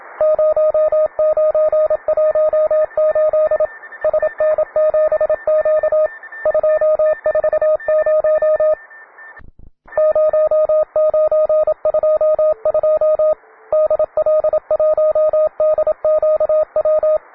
Morse code, heard as a keyed tone, about 27 wpm, 620 Hz, with radio noise. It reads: 0918 SN7Q 240 0922 DL1DQW